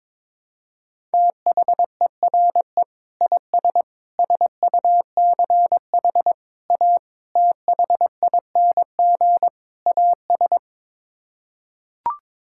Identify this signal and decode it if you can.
Morse code, heard as a keyed tone, about 22 wpm, 700 Hz. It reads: THERE IS SUCH A THING AS
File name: practice / 22wpm / Sets of 6 Words - 22wpm